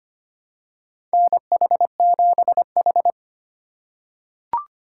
Morse code, heard as a keyed tone, about 25 wpm, 700 Hz. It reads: NH7H
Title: USA Call Signs - 2x Character Spacing - 25wpm